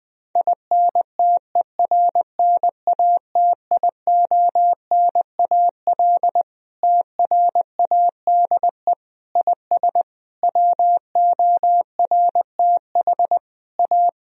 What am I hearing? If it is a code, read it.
Morse code, heard as a keyed tone, about 21 wpm, 700 Hz. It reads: INTERNATIONAL TRADE IS WORTH A
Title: Sentences from Top 1000 Words - Review Part 4 - 20wpm